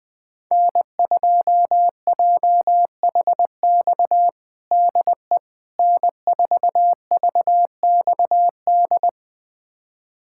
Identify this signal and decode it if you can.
Morse code, heard as a keyed tone, about 20 wpm, 700 Hz. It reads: N2JHX DE N4VXD
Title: USA Callsign DE Callsign - 5 Characters - 20wpm